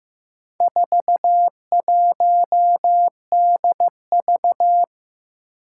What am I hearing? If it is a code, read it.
Morse code, heard as a keyed tone, about 16 wpm, 700 Hz. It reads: 41DV
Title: Four Challenging Letter-Number Combinations 15wpm